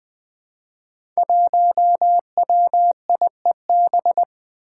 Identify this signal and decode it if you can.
Morse code, heard as a keyed tone, about 20 wpm, 700 Hz. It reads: 1WIEB
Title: Five Challenging Letter-Number Combinations 20wpm